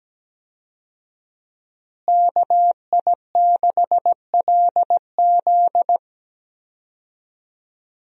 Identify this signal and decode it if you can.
Morse code, heard as a keyed tone, about 17 wpm, 700 Hz. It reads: KI6LZ
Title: USA Call Signs - 5 Characters - Repeated 3x - 17wpm